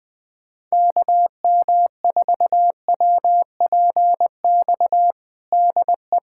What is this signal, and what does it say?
Morse code, heard as a keyed tone, about 21 wpm, 700 Hz. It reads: KM4WPX DE
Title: USA Callsign DE Callsign - 6 Characters - 20wpm